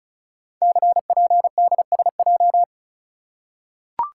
Morse code, heard as a keyed tone, about 33 wpm, 700 Hz. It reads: CPDSJ
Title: Code Groups - 5 Letter:Numbers - 35wpm